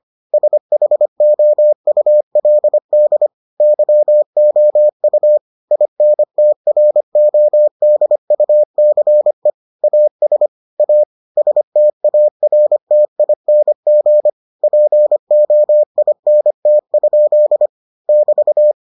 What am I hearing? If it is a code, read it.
Morse code, heard as a keyed tone, about 25 wpm, 600 Hz. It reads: SHOULD YOU INTRODUCE AS A STARTING POINT? <BT>